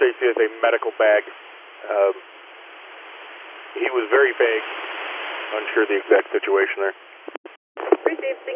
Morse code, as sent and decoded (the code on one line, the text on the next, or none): none